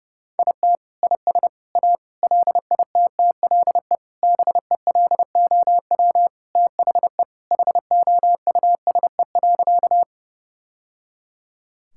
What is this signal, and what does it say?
Morse code, heard as a keyed tone, about 30 wpm, 700 Hz. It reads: IT IS A LITTLE BELOW THE HOUSE.
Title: Sentences from Top 1000 Words - Review Part 5 - 30wpm